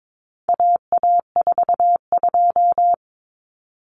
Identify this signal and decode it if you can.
Morse code, heard as a keyed tone, about 22 wpm, 700 Hz. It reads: AA42
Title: Common Cut Numbers - 4-Digit 22wpm